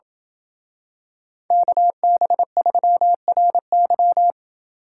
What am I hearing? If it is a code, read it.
Morse code, heard as a keyed tone, about 26 wpm, 700 Hz. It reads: KB3RY